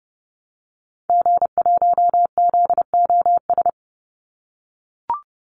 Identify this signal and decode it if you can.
Morse code, heard as a keyed tone, about 30 wpm, 700 Hz. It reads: G1ZOS
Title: Call Signs - 30wpm